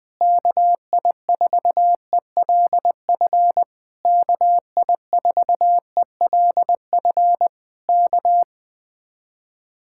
Morse code, heard as a keyed tone, about 20 wpm, 700 Hz. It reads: KI4ELF KI4ELF K
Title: Speed-Racing Common QSO Phrases Encore - 17wpm